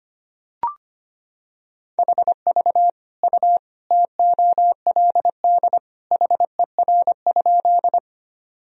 Morse code, heard as a keyed tone, about 25 wpm, 700 Hz. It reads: HV U TOLD HER?
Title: Sentences from Top 400 Words and Abbreviations 25wpm